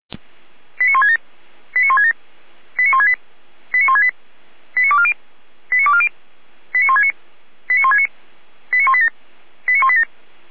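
Five-tone selective call (ZVEI), heard as quick runs of five tones, tones of about 70 ms. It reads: 89167 89167 89168 89168 89240 89240 89169 89169 89187 89187
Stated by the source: Five-tone ZVEI selective call received on SDR.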